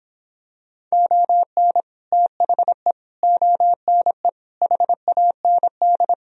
Morse code, heard as a keyed tone, about 26 wpm, 700 Hz. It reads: ON THE ONE HAND